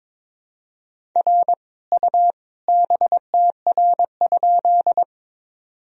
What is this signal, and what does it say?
Morse code, heard as a keyed tone, about 22 wpm, 700 Hz. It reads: R U BTR?
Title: Sentences from Top 400 Words and Abbreviations 22wpm